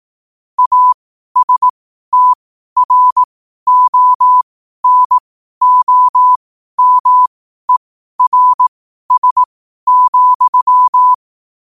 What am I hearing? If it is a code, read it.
Morse code, heard as a keyed tone, about 18 wpm, spaced as for 12 wpm, 1000 Hz. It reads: ASTRONOMERS,
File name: qotdfiles / 12wpm